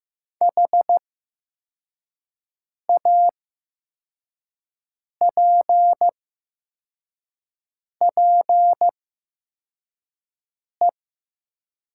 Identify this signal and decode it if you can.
Morse code, heard as a keyed tone, about 15 wpm, spaced as for 4 wpm, 700 Hz. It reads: HAPPE
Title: 8x Character Spacing - Top 500 Words - 15wpm